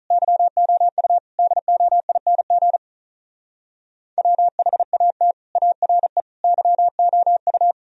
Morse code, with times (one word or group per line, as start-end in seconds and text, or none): 0.1-1.2 YOU
1.4-2.8 DOING
4.2-5.3 WHAT
5.6-6.2 ARE
6.4-7.7 YOU